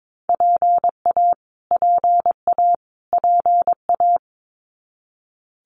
Morse code, heard as a keyed tone, about 22 wpm, 700 Hz. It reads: PA PA PA
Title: US States Repeated 3x 22wpm